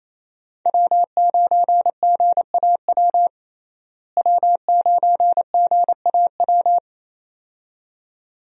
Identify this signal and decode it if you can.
Morse code, heard as a keyed tone, about 28 wpm, 700 Hz. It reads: W9GAW W9GAW
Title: USA Call Signs - 5 Characters - Repeated 2x - 28wpm